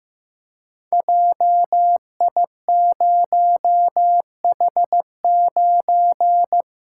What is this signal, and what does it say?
Morse code, heard as a keyed tone, about 15 wpm, 700 Hz. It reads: JI0H9
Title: Code Groups - 5 Letter:Numbers - 15wpm